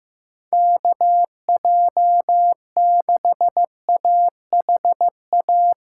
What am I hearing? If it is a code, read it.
Morse code, heard as a keyed tone, about 15 wpm, 700 Hz. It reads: KJ6AHA
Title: USA Call Signs - 6 Characters - 8x Character Spacing - 15wpm